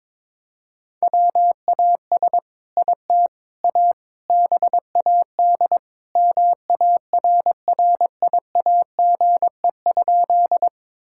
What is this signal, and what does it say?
Morse code, heard as a keyed tone, about 22 wpm, 700 Hz. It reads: WAS IT A BAD MARRIAGE?
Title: Sentences from Top 1000 Words - Review Part 5 - 22wpm